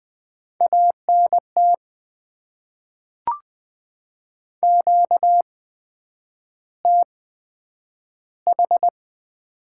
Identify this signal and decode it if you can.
Morse code, heard as a keyed tone, about 20 wpm, spaced as for 9 wpm, 700 Hz. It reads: ANT QTH
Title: Top 20 QSO Elements - 8x Character Spacing - 20wpm